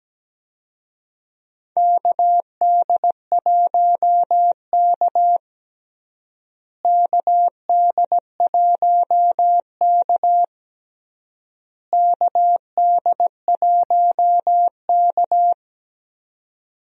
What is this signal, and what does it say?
Morse code, heard as a keyed tone, about 17 wpm, 700 Hz. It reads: KD1K KD1K KD1K
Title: USA Call Signs - 4 Characters - Repeated 3x - 17wpm